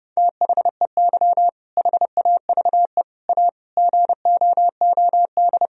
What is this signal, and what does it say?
Morse code, heard as a keyed tone, about 30 wpm, 700 Hz. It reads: THEY HAVE A GOOD
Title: Sentences from Top 1000 Words - Review Part 4 - 30wpm